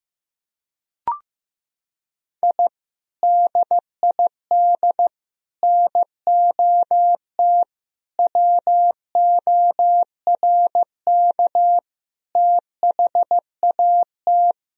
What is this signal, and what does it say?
Morse code, heard as a keyed tone, about 15 wpm, 700 Hz. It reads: I DID NOT WORK THAT
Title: Sentences from Top 1000 Words - Review Part 2 - 15wpm